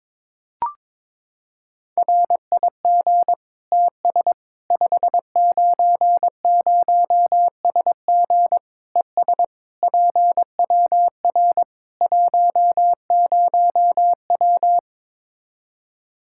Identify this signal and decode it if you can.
Morse code, heard as a keyed tone, about 22 wpm, 700 Hz. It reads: RIG TS 590SG ES PWR 10W
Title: Practice QSO - Exchange 2 - Info - 22wpm